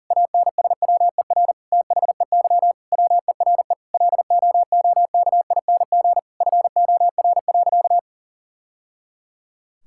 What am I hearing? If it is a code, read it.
Morse code, heard as a keyed tone, about 40 wpm, 700 Hz. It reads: ANSWER THEY WERE LOOKING FOR.
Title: Sentences from Top 1000 Words - Review Part 6 - 40wpm